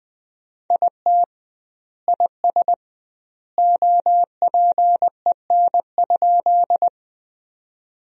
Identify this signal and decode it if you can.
Morse code, heard as a keyed tone, about 20 wpm, 700 Hz. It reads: IT IS OPEN?
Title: 2x Word-Spacing - Sentences from Top 300 Words - 20wpm